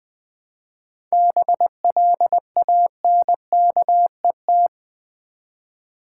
Morse code, heard as a keyed tone, about 20 wpm, 700 Hz. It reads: BLANKET
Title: Easy Words with Hints - 20wpm